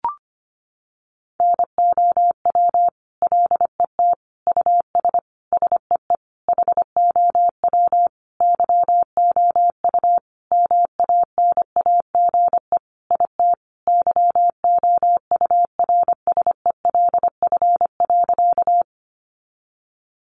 Morse code, heard as a keyed tone, about 25 wpm, 700 Hz. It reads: NOW LET US SEE HOW YOU MANAGE IT YOURSELF.
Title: Sentences from Top 1400 Words 25wpm